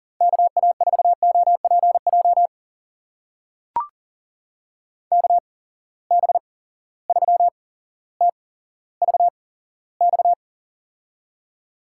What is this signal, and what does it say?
Morse code, heard as a keyed tone, about 40 wpm, spaced as for 17 wpm, 700 Hz. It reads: KA4OPJ KB3TVX
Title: USA Call Signs - 6 Characters - 8x Character Spacing - 40wpm